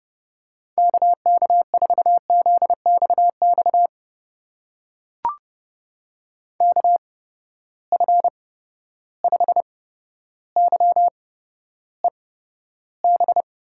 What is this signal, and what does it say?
Morse code, heard as a keyed tone, about 32 wpm, spaced as for 13 wpm, 700 Hz. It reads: KK4ZXX KF5YEB
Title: USA Call Signs - 6 Characters - 8x Character Spacing - 30wpm